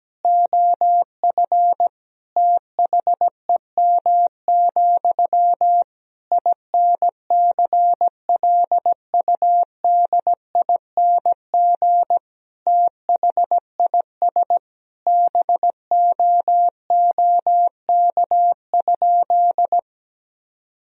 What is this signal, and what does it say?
Morse code, heard as a keyed tone, about 17 wpm, 700 Hz. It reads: OF THEM, INCLUDING THIS BOOK?
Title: Sentences from Top 1000 Words - Review Part 4 - 17wpm